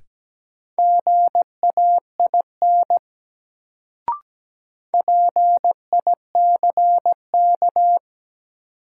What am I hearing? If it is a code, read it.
Morse code, heard as a keyed tone, about 17 wpm, 700 Hz. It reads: GAIN PICK
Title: Sentences from Top 1000 Words - New Word Drill - 17wpm